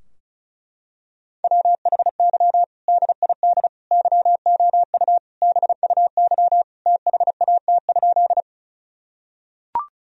Morse code, heard as a keyed tone, about 36 wpm, 700 Hz. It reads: WHY DID YOU BUY THAT?
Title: Sentences from Top 1000 Words - Review Part 4 - 35wpm